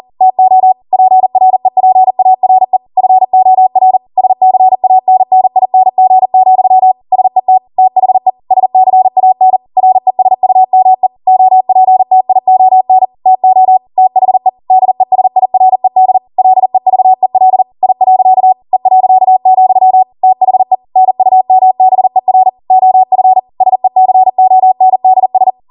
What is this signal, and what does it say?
Morse code, heard as a keyed tone, about 40 wpm, 750 Hz. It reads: TO PREPARE FOR SCANNING, SET THE SCAN RESUME OPTION TO THE DESIRED LEVEL I. E., THE NUMBER OF SECONDS
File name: Morse / 40WPM